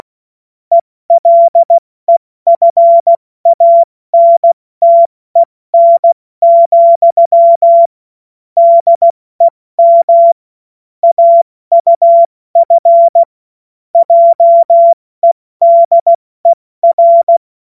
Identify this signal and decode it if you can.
Morse code, heard as a keyed tone, about 16 wpm, 680 Hz. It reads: ELEFANTEN, DEM AUF JEDER
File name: Morsekurs / Witze / Witz 27. Januar 2026 70 F80